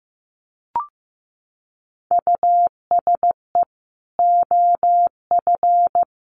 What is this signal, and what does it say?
Morse code, heard as a keyed tone, about 15 wpm, 700 Hz. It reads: USE OF
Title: Sets of 2 Words 15wpm